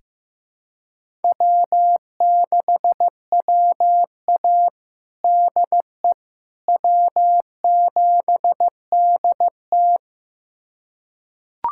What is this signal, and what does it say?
Morse code, heard as a keyed tone, about 15 wpm, 700 Hz. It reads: W6WA DE W7DT
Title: USA Callsign DE Callsign - 4 Characters - 15wpm